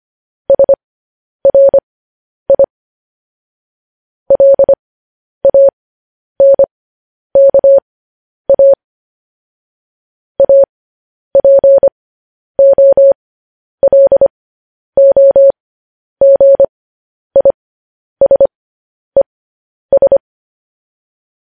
Morse code, heard as a keyed tone, about 25 wpm, spaced as for 10 wpm, 550 Hz. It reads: SRI LANKA APOLOGISES